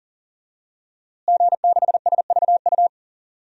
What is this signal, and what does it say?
Morse code, heard as a keyed tone, about 40 wpm, 700 Hz. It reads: G6SVU
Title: DXCC-100 Call Signs - 5 Characters - Repeated 3x - 40wpm